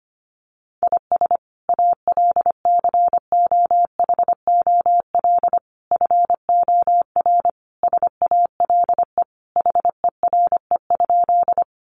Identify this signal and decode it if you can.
Morse code, heard as a keyed tone, about 25 wpm, 700 Hz. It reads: IS ALCOHOL FOR SALE HERE?
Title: Sentences from Top 1400 Words 25wpm